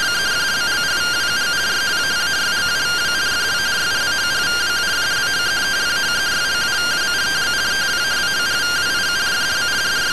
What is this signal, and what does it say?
Two-tone teleprinter signal, about 170 Hz shift.
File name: findings / 81 axial iodine 262.gen-20 rtty rx